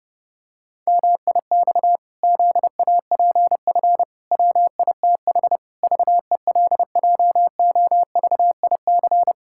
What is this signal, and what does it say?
Morse code, heard as a keyed tone, about 30 wpm, 700 Hz. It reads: MIX ZAPF WITH VELJOVIC
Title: Pangram Challenge v2 - 30wpm